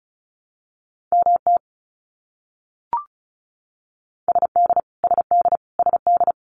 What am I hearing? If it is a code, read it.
Morse code, heard as a keyed tone, about 34 wpm, 700 Hz. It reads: MT SD SD SD
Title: US States Repeated 3x 35wpm